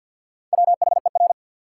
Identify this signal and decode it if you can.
Morse code, heard as a keyed tone, about 52 wpm, 700 Hz. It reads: WHER
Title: Sentences from Top 1500 Words - New Word Drill - 50wpm